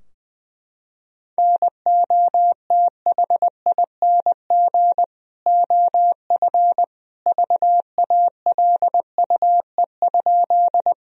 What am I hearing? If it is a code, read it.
Morse code, heard as a keyed tone, about 21 wpm, 700 Hz. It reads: NOTHING OF VALUE?
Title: Sentences from Top 1000 Words - Review Part 3 - 20wpm